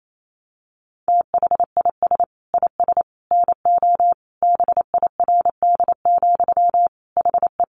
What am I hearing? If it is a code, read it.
Morse code, heard as a keyed tone, about 28 wpm, 700 Hz. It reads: THIS IS NO BIRD, HE
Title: Sentences from Top 700 Words 28wpm